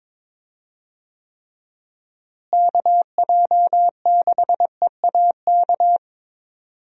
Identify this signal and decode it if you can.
Morse code, heard as a keyed tone, about 22 wpm, 700 Hz. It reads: KJ6EAK
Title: USA Call Signs - 6 Characters - Repeated 2x - 22wpm